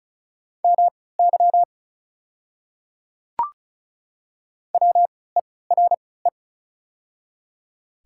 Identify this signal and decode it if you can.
Morse code, heard as a keyed tone, about 34 wpm, spaced as for 20 wpm, 700 Hz. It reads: MY WERE
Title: 3x Character Spacing - Top 100 Words - 35wpm